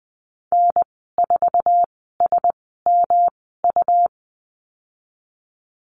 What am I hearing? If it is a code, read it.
Morse code, heard as a keyed tone, about 20 wpm, spaced as for 15 wpm, 700 Hz. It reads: N4SMU
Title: 154 Lesson 25g - Letter p - Call Signs 20wpm